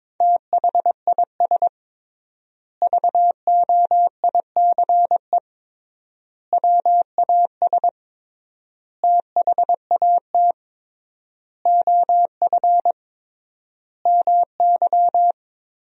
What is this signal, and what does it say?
Morse code, heard as a keyed tone, about 22 wpm, 700 Hz. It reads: THIS VOICE WAS THAT OF MY
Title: 3x Word-Spacing - Sentences from Top 500 Words - 22wpm